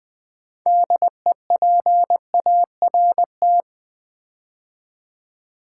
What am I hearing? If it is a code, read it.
Morse code, heard as a keyed tone, about 20 wpm, 700 Hz. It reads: DEPART
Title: Build A Word - 3 Letters At A Time - 9-Letter Words - 20wpm